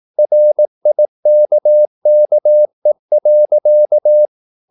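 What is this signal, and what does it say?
Morse code, heard as a keyed tone, about 18 wpm, 600 Hz. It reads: RIKKE.
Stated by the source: Slow (QRS) Morse practice at 18 wpm, CW med Gnister.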